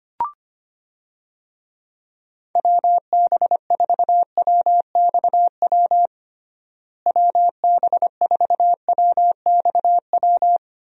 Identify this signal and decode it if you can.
Morse code, heard as a keyed tone, about 25 wpm, 700 Hz. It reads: WB4WXW WB4WXW
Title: USA Call Signs - 6 Characters - Repeated 2x - 25wpm